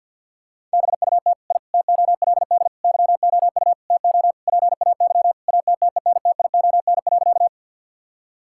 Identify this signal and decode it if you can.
Morse code, heard as a keyed tone, about 50 wpm, 700 Hz. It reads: BUT I TOLD YOU TO PAY ATTENTION.
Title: Sentences from Top 1000 Words - Review Part 6 - 50wpm